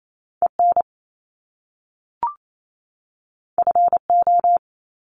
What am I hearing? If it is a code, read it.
Morse code, heard as a keyed tone, about 28 wpm, 700 Hz. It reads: EN FO
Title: Top Two Letter Combinations in Words 28wpm